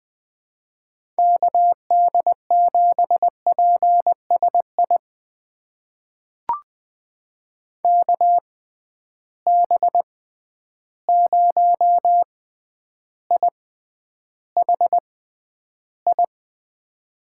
Morse code, heard as a keyed tone, about 20 wpm, spaced as for 10 wpm, 700 Hz. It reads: KD7PSI KB0IHI
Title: USA Call Signs - 6 Characters - 6x Character Spacing - 20wpm